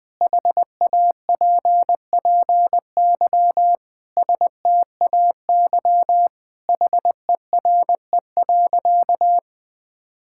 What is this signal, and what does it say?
Morse code, heard as a keyed tone, about 21 wpm, 700 Hz. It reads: HAPPY STAY HERE.
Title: Sentences with 4-Word Phrases - 20wpm